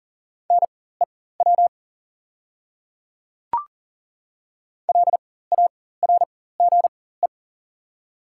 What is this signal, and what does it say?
Morse code, heard as a keyed tone, about 41 wpm, spaced as for 19 wpm, 700 Hz. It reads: NEW LARGE
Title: 4x Character Spacing - Top 200 Words - 40wpm